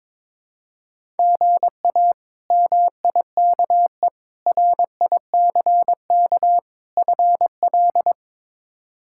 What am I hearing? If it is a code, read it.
Morse code, heard as a keyed tone, about 22 wpm, 700 Hz. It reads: GA MIKE RICK FL
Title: K1USN SST QSO Practice - 22wpm